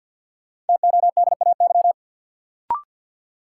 Morse code, heard as a keyed tone, about 48 wpm, 700 Hz. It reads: TODAY
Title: Sentences from Top 600 Words - New Word Drill - 50wpm